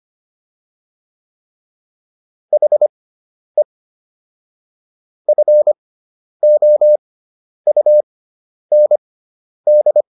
Morse code, heard as a keyed tone, about 25 wpm, spaced as for 10 wpm, 600 Hz. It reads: HE FOUND